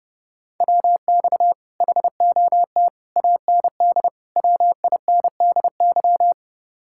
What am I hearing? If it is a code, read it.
Morse code, heard as a keyed tone, about 30 wpm, 700 Hz. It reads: WX HOT AND WINDY